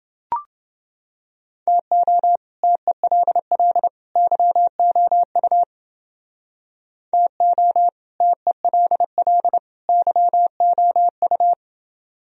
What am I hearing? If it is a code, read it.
Morse code, heard as a keyed tone, about 28 wpm, 700 Hz. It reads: TO TELL YOU TO TELL YOU